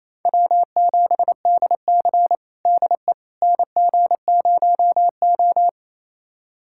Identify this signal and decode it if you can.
Morse code, heard as a keyed tone, about 28 wpm, 700 Hz. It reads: W7DC DE NG0O
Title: USA Callsign DE Callsign - 4 Characters - 28wpm